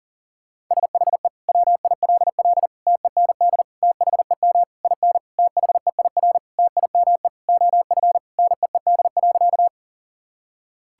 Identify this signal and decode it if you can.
Morse code, heard as a keyed tone, about 40 wpm, 690 Hz. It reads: SHE WILL TEND THEM IN THEIR TIME OF NEED.
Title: Sentences from Top 1000 Words - Review Part 5 - 40wpm